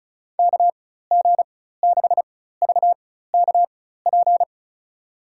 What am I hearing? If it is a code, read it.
Morse code, heard as a keyed tone, about 36 wpm, spaced as for 16 wpm, 700 Hz. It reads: KG6VKP
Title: USA Call Signs - 6 Characters - 4x Character Spacing - 35wpm